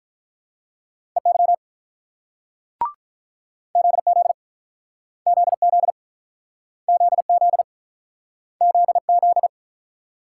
Morse code, heard as a keyed tone, about 47 wpm, 700 Hz. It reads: EY ZZ ZZ ZZ ZZ